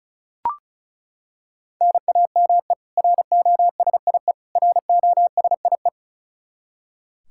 Morse code, heard as a keyed tone, about 34 wpm, 700 Hz. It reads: NAME ROSIE ROSIE
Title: K1USN SST QSO Practice 35wpm